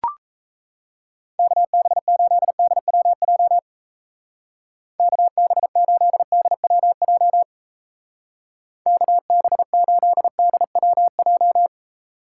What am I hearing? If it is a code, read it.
Morse code, heard as a keyed tone, about 34 wpm, 700 Hz. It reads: KB8DWJ KB8DWJ KB8DWJ
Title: Speed Racing - USA 2x3 Call Signs 28wpm